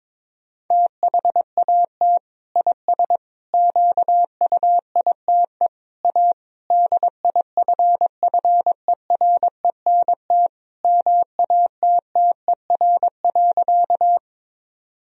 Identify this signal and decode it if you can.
Morse code, heard as a keyed tone, about 22 wpm, 700 Hz. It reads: THAT IS QUITE A DIFFERENT MATTER.
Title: Sentences from Top 1000 Words - Review Part 3 - 22wpm